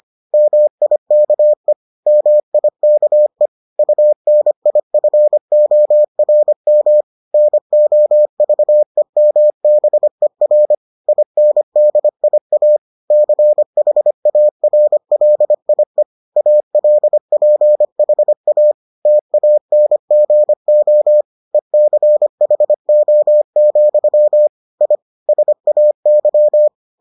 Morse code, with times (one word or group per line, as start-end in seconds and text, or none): 0.3-1.7 MIKE
2.1-3.5 MIKE
3.8-7.0 UNIFORM
7.3-10.7 NOVEMBER
11.1-12.8 INDIA
13.1-16.0 CHARLIE
16.4-18.7 ALPHA
19.1-21.2 TANGO
21.5-24.5 ECHO,
24.8-24.9 I
25.3-26.7 SAY